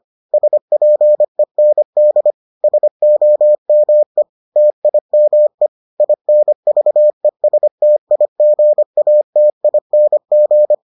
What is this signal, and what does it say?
Morse code, heard as a keyed tone, about 25 wpm, 600 Hz. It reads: SPEND SOME TIME INVESTIGATING